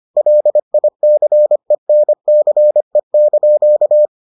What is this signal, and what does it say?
Morse code, heard as a keyed tone, about 25 wpm, 600 Hz. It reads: LICENCE)